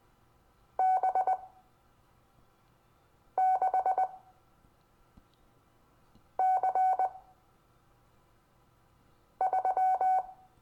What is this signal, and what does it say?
Morse code, heard as a keyed tone, about 20 wpm, spaced as for 4 wpm, 750 Hz. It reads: B6C3
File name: cw / Lesson25 6 X